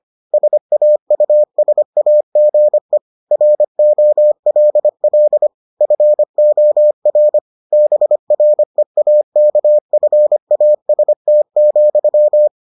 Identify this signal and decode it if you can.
Morse code, heard as a keyed tone, about 25 wpm, 600 Hz. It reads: SAUSAGE ROLL FOR BREAKFAST,